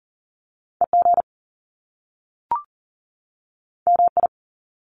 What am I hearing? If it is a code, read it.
Morse code, heard as a keyed tone, about 40 wpm, 700 Hz. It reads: EG MI
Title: Top Two Letter Combinations in Words 40wpm